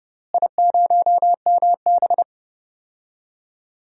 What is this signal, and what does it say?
Morse code, heard as a keyed tone, about 32 wpm, 700 Hz. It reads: I0MB
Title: DXCC-100 Call Signs - 4 Characters - Repeated 3x - 30wpm